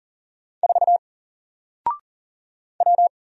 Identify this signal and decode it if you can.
Morse code, heard as a keyed tone, about 40 wpm, 700 Hz. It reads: 4 W
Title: Single Character (Letter:Number:Punctuation:Prosign) 40wpm